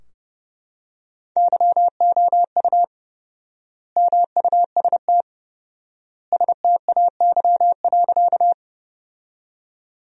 Morse code, heard as a keyed tone, about 31 wpm, 700 Hz. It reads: YOU MUST STAY.
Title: 4x Word-Spacing - Sentences from Top 500 Words - 30wpm